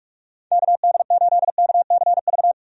Morse code, heard as a keyed tone, about 46 wpm, 700 Hz. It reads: KD8KCV
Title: USA Call Signs - 6 Characters - 45wpm